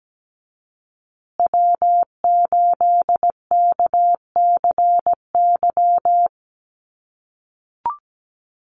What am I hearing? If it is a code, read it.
Morse code, heard as a keyed tone, about 17 wpm, 700 Hz. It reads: W8KCY
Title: USA Call Signs Repeated 3x 17wpm